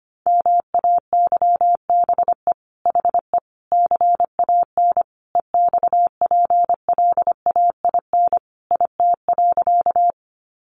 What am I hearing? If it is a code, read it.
Morse code, heard as a keyed tone, about 25 wpm, 700 Hz. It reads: MAYBE HE CAN EXPLAIN IT.
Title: Sentences from Top 1100 Words 25wpm